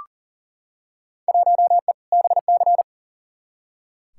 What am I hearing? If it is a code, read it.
Morse code, heard as a keyed tone, about 40 wpm, 700 Hz. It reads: 1E BC
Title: Field Day Exchanges - 40wpm